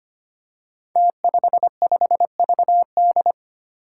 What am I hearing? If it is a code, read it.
Morse code, heard as a keyed tone, about 25 wpm, 700 Hz. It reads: T55VD
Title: Five Challenging Letter-Number Combinations 25wpm